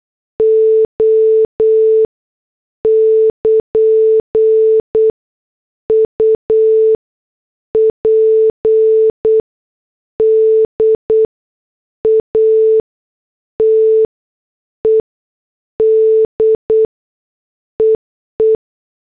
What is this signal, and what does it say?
Morse code, heard as a keyed tone, about 8 wpm, spaced as for 6 wpm, 440 Hz. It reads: O <KN> UPDATED EE
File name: morse / headlines